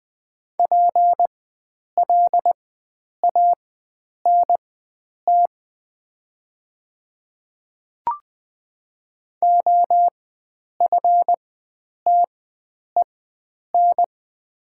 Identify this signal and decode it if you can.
Morse code, heard as a keyed tone, about 20 wpm, spaced as for 9 wpm, 700 Hz. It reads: PLANT OFTEN
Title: 4x Character Spacing - Top 300 Words - 20wpm